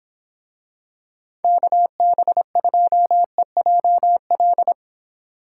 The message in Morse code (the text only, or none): KB2EJL